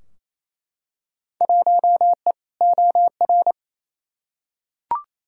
Morse code, heard as a keyed tone, about 28 wpm, 700 Hz. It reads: 1E OR
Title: Field Day Exchanges Encore - 28wpm